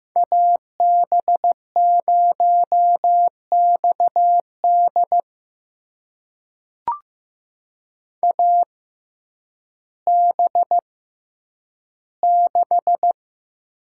Morse code, heard as a keyed tone, about 15 wpm, spaced as for 9 wpm, 700 Hz. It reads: AB0XD AB6
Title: USA Call Signs - 5 Characters - 6x Character Spacing - 15wpm